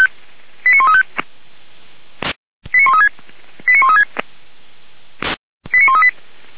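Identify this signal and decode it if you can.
Five-tone selective call (ZVEI), heard as quick runs of five tones, tones of about 70 ms. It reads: 89135 89136 89136 89138
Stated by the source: ZVEI five-tone selective call 89134 received by SDR.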